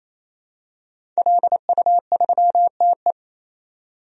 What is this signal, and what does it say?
Morse code, heard as a keyed tone, about 28 wpm, 700 Hz. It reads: LU3TE
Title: DXCC-100 Call Signs - 5 Characters - 28wpm